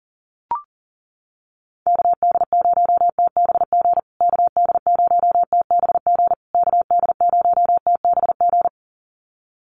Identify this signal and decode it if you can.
Morse code, heard as a keyed tone, about 40 wpm, 700 Hz. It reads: KD0TBG KD0TBG KD0TBG
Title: USA Call Signs Repeated 3x 40wpm